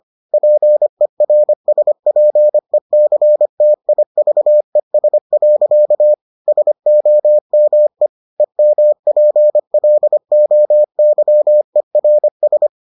Morse code, heard as a keyed tone, about 25 wpm, 600 Hz. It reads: PERSPECTIVES. SOME EMPLOYERS